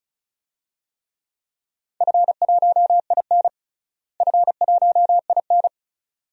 Morse code, heard as a keyed tone, about 34 wpm, 700 Hz. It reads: F1IN F1IN
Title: DXCC-100 Call Signs - 4 Characters - Repeated 2x - 35wpm